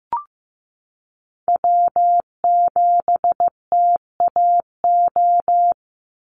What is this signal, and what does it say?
Morse code, heard as a keyed tone, about 15 wpm, 700 Hz. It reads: W7TAO
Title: NA SOTA Chaser Callsigns 15wpm